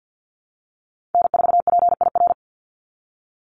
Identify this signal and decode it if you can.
Morse code, heard as a keyed tone, about 50 wpm, 700 Hz. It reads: N4PIR
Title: Call Signs Repeated 2x - 50wpm